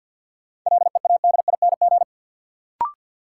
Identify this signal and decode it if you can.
Morse code, heard as a keyed tone, about 53 wpm, 700 Hz. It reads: LEADING
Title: Sentences from Top 1100 Words - New Word Drill - 50wpm